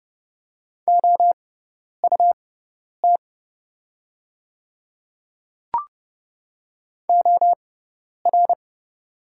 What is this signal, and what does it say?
Morse code, heard as a keyed tone, about 29 wpm, spaced as for 10 wpm, 700 Hz. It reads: OUT OR
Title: 6x Character Spacing - Top 100 Words - 30wpm